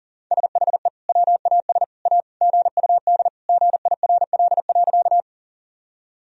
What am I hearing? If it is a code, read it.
Morse code, heard as a keyed tone, about 40 wpm, 700 Hz. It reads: SHE WAS A GUD GIRL.
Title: Sentences from Top 400 Words and Abbreviations 40wpm